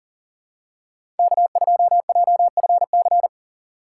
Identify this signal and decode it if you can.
Morse code, heard as a keyed tone, about 40 wpm, 700 Hz. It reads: K2JFC
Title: USA Call Signs - 5 Characters - 6x Character Spacing - 40wpm